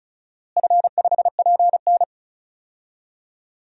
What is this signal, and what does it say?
Morse code, heard as a keyed tone, about 36 wpm, 700 Hz. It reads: F5PN
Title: DXCC-100 Call Signs - 4 Characters - Repeated 2x - 35wpm